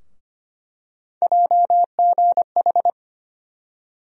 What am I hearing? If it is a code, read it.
Morse code, heard as a keyed tone, about 25 wpm, 700 Hz. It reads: JGH
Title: USA Call Sign Postfixes - 3 Letters 25wpm